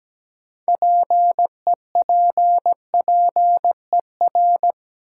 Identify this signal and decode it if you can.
Morse code, heard as a keyed tone, about 17 wpm, 700 Hz. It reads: PEPPER
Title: Sets of 5 Words - New Word Drill - 17wpm